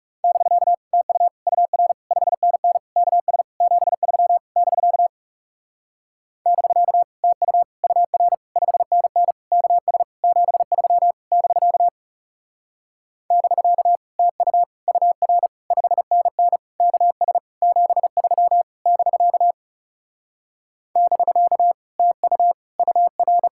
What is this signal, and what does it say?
Morse code, heard as a keyed tone, about 40 wpm, 700 Hz. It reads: <BK> TU UR 5NN KS 73 <BK> <BK> TU UR 5NN KS 73 <BK> <BK> TU UR 5NN KS 73 <BK> <BK> TU UR